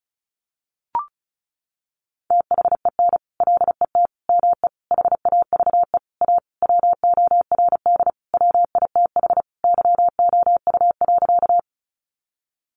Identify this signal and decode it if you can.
Morse code, heard as a keyed tone, about 34 wpm, 700 Hz. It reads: THEN LET ME HAVE A WORD WITH YOU.
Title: Sentences from Top 300 Words 35wpm